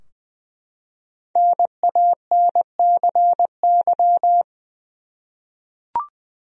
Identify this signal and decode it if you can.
Morse code, heard as a keyed tone, about 20 wpm, 700 Hz. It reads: NANCY
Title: USA Common Names - 20wpm